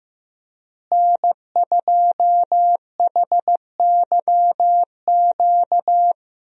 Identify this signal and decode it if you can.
Morse code, heard as a keyed tone, about 16 wpm, 700 Hz. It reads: N2HYQ
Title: USA Call Signs - 5 Characters - Repeated 2x - 15wpm